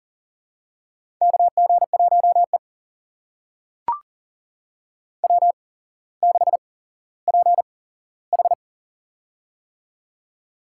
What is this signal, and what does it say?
Morse code, heard as a keyed tone, about 42 wpm, spaced as for 17 wpm, 690 Hz. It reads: KG1E W6PH
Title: USA Call Signs - 4 Characters - 8x Character Spacing - 40wpm